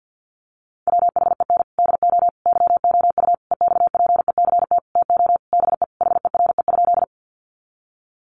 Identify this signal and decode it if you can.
Morse code, heard as a keyed tone, about 50 wpm, 700 Hz. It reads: WHEN DO YOU EXPECT TO BE HERE?
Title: Sentences from Top 1100 Words 50wpm